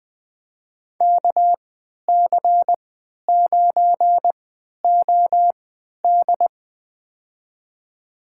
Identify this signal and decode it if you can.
Morse code, heard as a keyed tone, about 20 wpm, spaced as for 11 wpm, 700 Hz. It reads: KC9OD
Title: USA Call Signs - 5 Characters - 3x Character Spacing - 20wpm